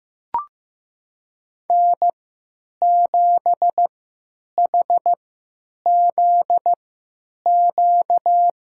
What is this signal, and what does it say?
Morse code, heard as a keyed tone, about 15 wpm, spaced as for 9 wpm, 700 Hz. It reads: N7HZQ
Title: USA Call Signs - 5 Characters - 3x Character Spacing - 15wpm